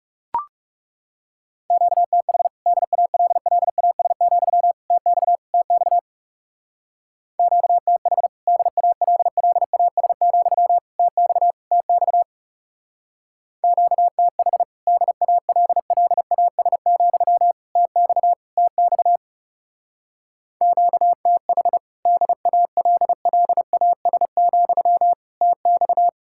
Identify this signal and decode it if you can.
Morse code, heard as a keyed tone, about 37 wpm, 700 Hz. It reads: QTH DALLAS, TX TX QTH DALLAS, TX TX QTH DALLAS, TX TX QTH DALLAS, TX
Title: Speed-Racing Common QSO Phrases - 30wpm